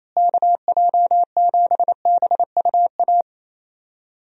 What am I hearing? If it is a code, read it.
Morse code, heard as a keyed tone, about 28 wpm, 700 Hz. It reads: KJ7BUA
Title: USA Call Signs - 6 Characters - Repeated 3x - 28wpm